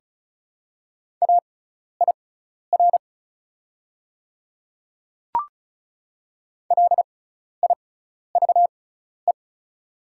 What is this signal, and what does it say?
Morse code, heard as a keyed tone, about 34 wpm, spaced as for 12 wpm, 700 Hz. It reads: AIR LIVE
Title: 6x Character Spacing - Top 200 Words - 35wpm